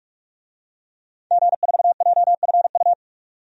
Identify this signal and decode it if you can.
Morse code, heard as a keyed tone, about 46 wpm, 700 Hz. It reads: G4JFU
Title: DXCC-100 Call Signs - 5 Characters - 45wpm